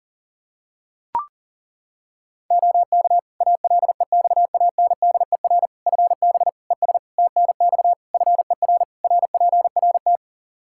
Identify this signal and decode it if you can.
Morse code, heard as a keyed tone, about 37 wpm, 700 Hz. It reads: OK ALEXANDER FB ES TNX FER RPRT
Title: Practice QSO - Exchange 2 - Info - 40wpm